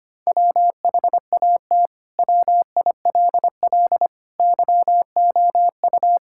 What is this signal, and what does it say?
Morse code, heard as a keyed tone, about 25 wpm, 700 Hz. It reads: WHAT WILL YOU
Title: Sentences from Top 1000 Words - Review Part 6 - 25wpm